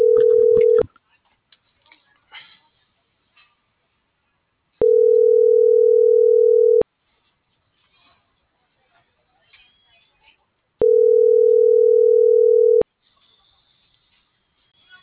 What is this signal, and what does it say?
Ringback tone (North America), 440 + 480 Hz.